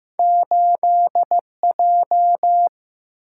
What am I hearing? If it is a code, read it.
Morse code, heard as a keyed tone, about 15 wpm, 700 Hz. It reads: 8 J
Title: Single-Letter Single-Number 15wpm